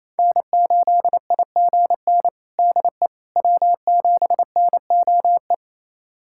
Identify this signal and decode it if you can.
Morse code, heard as a keyed tone, about 28 wpm, 700 Hz. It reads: N8IGN DE W7NOE
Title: USA Callsign DE Callsign - 5 Characters - 28wpm